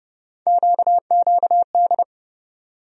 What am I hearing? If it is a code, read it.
Morse code, heard as a keyed tone, about 30 wpm, 700 Hz. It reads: QQD
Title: USA Call Sign Postfixes - 3 Letters 30wpm